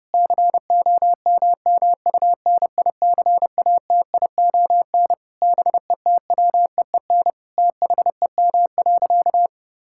Morse code, heard as a keyed tone, about 30 wpm, 700 Hz. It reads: COMMUNICATION BETWEEN THEM.
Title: Sentences from Top 1000 Words - Review Part 6 - 30wpm